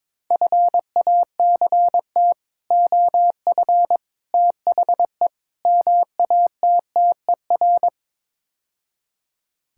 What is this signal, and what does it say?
Morse code, heard as a keyed tone, about 22 wpm, 700 Hz. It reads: FACT OF THE MATTER
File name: practice / 22wpm / Build A Phrase - 5-Word Phrases - 22wpm